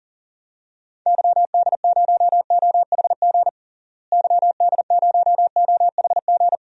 Morse code, heard as a keyed tone, about 40 wpm, 700 Hz. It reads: YD0OHG YD0OHG
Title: DXCC-100 Call Signs - 6 Characters - Repeated 2x - 40wpm